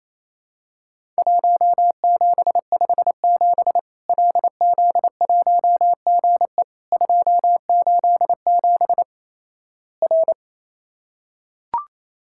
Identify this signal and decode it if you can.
Morse code, heard as a keyed tone, about 27 wpm, 700 Hz. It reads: 1757 LZ1GE 287 R
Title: Worked All Europe DX Contest - QTC Exchange - 28wpm